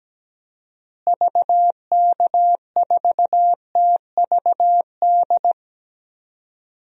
Morse code, heard as a keyed tone, about 17 wpm, 700 Hz. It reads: VK4TVD
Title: DXCC-100 Call Signs - 6 Characters - Repeated 2x - 17wpm